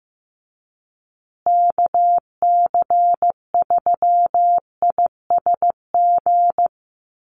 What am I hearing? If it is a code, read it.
Morse code, heard as a keyed tone, about 15 wpm, 700 Hz. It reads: KC3ISG
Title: USA Call Signs Repeated 3x 15wpm